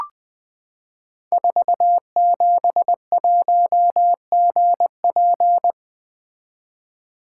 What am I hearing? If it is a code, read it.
Morse code, heard as a keyed tone, about 20 wpm, 700 Hz. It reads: 471GP
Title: Code Groups - 5 Letter:Numbers - 20wpm